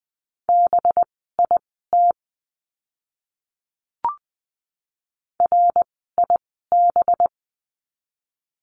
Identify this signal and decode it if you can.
Morse code, heard as a keyed tone, about 20 wpm, spaced as for 14 wpm, 700 Hz. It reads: BIT RIB
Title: 185 Lesson 30f - Letter b - Words 20wpm